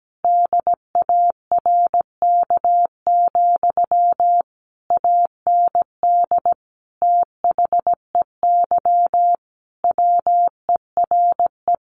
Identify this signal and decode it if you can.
Morse code, heard as a keyed tone, about 17 wpm, 700 Hz. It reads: DARK, AND THEY WERE
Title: Sentences from Top 1400 Words 17wpm